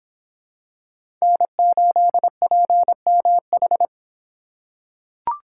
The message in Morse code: N8PMH

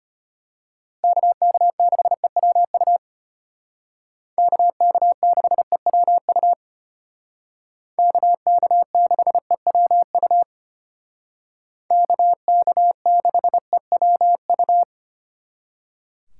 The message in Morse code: KK6EWU KK6EWU KK6EWU KK6EWU

